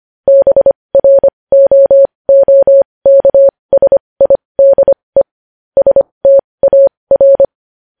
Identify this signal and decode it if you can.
Morse code, heard as a keyed tone, about 25 wpm, spaced as for 20 wpm, 550 Hz. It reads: BROOKSIDE STAR
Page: News Headlines in Morse Code at 20 WPM